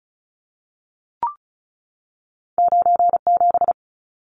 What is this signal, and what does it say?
Morse code, heard as a keyed tone, about 36 wpm, 700 Hz. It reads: 97
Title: Numbers - 2-Digit 35wpm